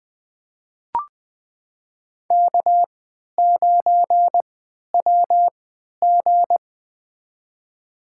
Morse code, heard as a keyed tone, about 20 wpm, spaced as for 11 wpm, 700 Hz. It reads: K9WG
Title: USA Call Signs - 4 Characters - 3x Character Spacing - 20wpm